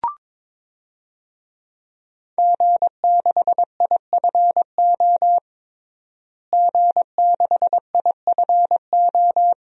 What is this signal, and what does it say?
Morse code, heard as a keyed tone, about 22 wpm, 700 Hz. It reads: G6IFO G6IFO
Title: DXCC-100 Call Signs - 5 Characters - Repeated 3x - 22wpm